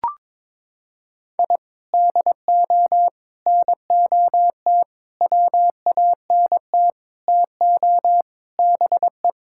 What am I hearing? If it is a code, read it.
Morse code, heard as a keyed tone, about 22 wpm, 700 Hz. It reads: I DO NOT WANT TO BE
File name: practice / 22wpm / Sentences with 4-Word Phrases - 22wpm